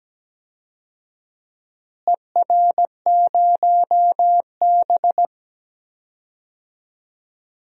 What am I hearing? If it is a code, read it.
Morse code, heard as a keyed tone, about 17 wpm, 700 Hz. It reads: ER0B
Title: DXCC-100 Call Signs - 4 Characters - Repeated 2x - 17wpm